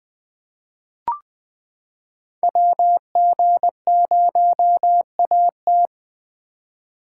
Morse code, heard as a keyed tone, about 20 wpm, 700 Hz. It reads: WG0AT
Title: 2023 SOTA Chaser Call Signs - 20wpm